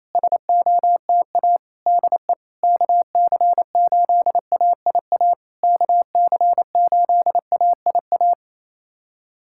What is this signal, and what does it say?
Morse code, heard as a keyed tone, about 28 wpm, 700 Hz. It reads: SOTA DE KC8AIA KC8AIA
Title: SOTA QSO Practice 28wpm